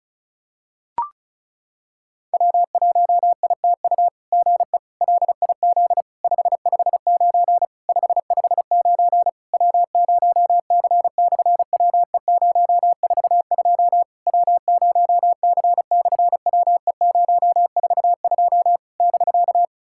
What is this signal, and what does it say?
Morse code, heard as a keyed tone, about 36 wpm, 700 Hz. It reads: W1ITU GE LIZ 559 559 W0C/WE042 W0C/WE042 <BK>
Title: SOTA QSO Practice 35wpm